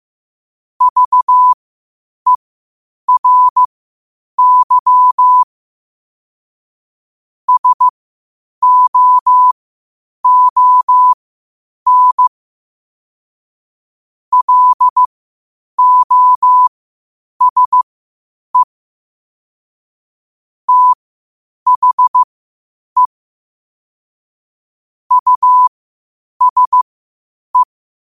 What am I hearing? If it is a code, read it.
Morse code, heard as a keyed tone, about 15 wpm, spaced as for 8 wpm, 1000 Hz. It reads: VERY SOON LOSE THE USE
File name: qotdfiles / 8wpm